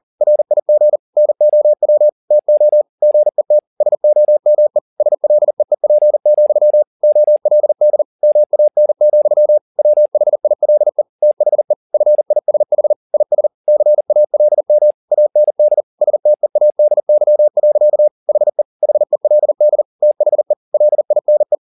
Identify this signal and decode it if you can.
Morse code, heard as a keyed tone, about 40 wpm, 600 Hz. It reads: RIG NOW TO GET SOME SLEEP, OLD MAN, WHILE THE FISH IS CALM AND STEADY. HE HELD THE LINE
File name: the old man and the sea 2 40wpm